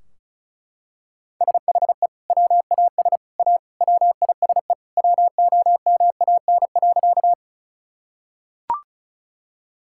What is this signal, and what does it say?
Morse code, heard as a keyed tone, about 36 wpm, 700 Hz. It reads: SHE WAS A WISE WOMAN.
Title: Sentences from Top 1600 Words 35wpm